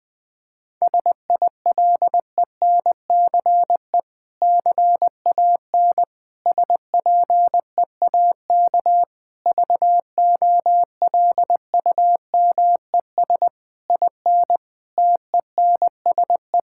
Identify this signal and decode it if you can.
Morse code, heard as a keyed tone, about 20 wpm, 700 Hz. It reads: SILENCE CAN SPEAK VOLUMES IN TENSE
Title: Difficult Sentences with Hints - 20wpm